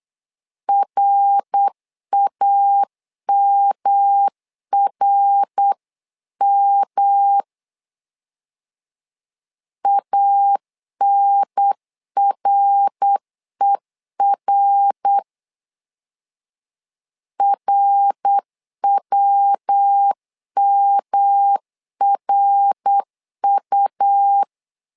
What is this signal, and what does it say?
Morse code, heard as a keyed tone, about 8 wpm, 800 Hz. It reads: RAMRM ANRER RWMRU
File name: LEZIONI / 3 MWRO / 3 MWR 8wpm 2444